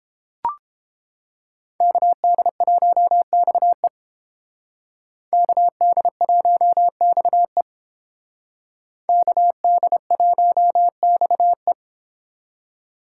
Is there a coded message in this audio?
Yes: Morse code from 1.8 to 11.7 s, heard as a keyed tone, about 28 wpm, 700 Hz. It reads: KD1XE KD1XE KD1XE